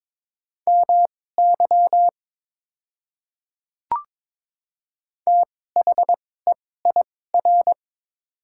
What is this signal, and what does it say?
Morse code, heard as a keyed tone, about 22 wpm, spaced as for 16 wpm, 690 Hz. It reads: MY THEIR
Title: 2x Character Spacing - Top 100 Words - 22wpm